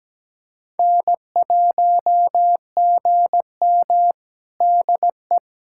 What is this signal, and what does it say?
Morse code, heard as a keyed tone, about 17 wpm, 700 Hz. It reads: N1GM DE
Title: USA Callsign DE Callsign - 4 Characters - 17wpm